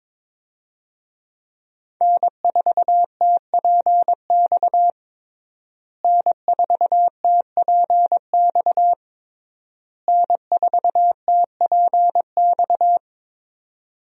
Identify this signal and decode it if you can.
Morse code, heard as a keyed tone, about 22 wpm, 700 Hz. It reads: N4TPX N4TPX N4TPX
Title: USA Call Signs - 5 Characters - Repeated 3x - 22wpm